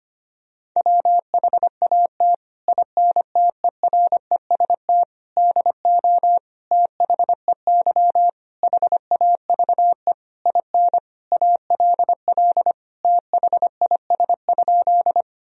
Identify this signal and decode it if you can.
Morse code, heard as a keyed tone, about 25 wpm, 700 Hz. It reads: WHAT INTEREST DO THEY HAVE IN ALL THIS?
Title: Sentences from Top 1000 Words - Review Part 2 - 25wpm